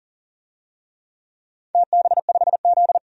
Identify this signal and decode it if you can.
Morse code, heard as a keyed tone, about 40 wpm, 700 Hz. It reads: TB5Z
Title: DXCC-100 Call Signs - 4 Characters - 40wpm